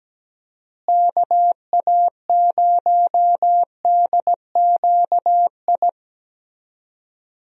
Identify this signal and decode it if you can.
Morse code, heard as a keyed tone, about 17 wpm, 700 Hz. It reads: KA0DQI
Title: USA Call Signs - 6 Characters - Repeated 2x - 17wpm